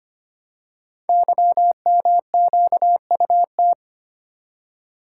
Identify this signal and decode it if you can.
Morse code, heard as a keyed tone, about 25 wpm, 700 Hz. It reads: YMQUT
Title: Code Groups - 5 Letters - 25wpm